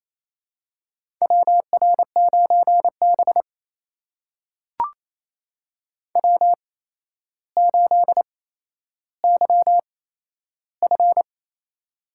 Morse code, heard as a keyed tone, about 29 wpm, spaced as for 12 wpm, 700 Hz. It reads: WR9B W8YF
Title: USA Call Signs - 4 Characters - 8x Character Spacing - 28wpm